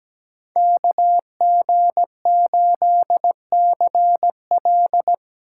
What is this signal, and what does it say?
Morse code, heard as a keyed tone, about 20 wpm, 700 Hz. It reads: KG8CL
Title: Speed Racing - USA 2x2 Call Signs 17wpm